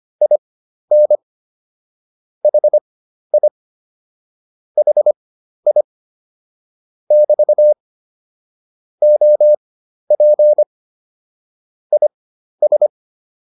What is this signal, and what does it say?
Morse code, heard as a keyed tone, about 25 wpm, spaced as for 12 wpm, 600 Hz. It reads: IN HI HI <BT> OP IS